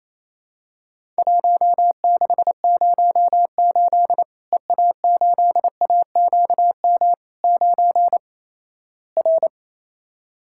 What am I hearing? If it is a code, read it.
Morse code, heard as a keyed tone, about 28 wpm, 700 Hz. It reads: 1608 EA8AQM 9 R
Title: Worked All Europe DX Contest - QTC Exchange - 28wpm